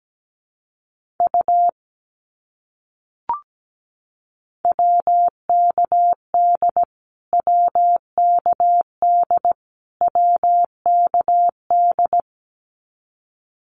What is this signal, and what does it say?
Morse code, heard as a keyed tone, about 17 wpm, 700 Hz. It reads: U WKD WKD WKD
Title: Abbreviations Repeated 3x - 17wpm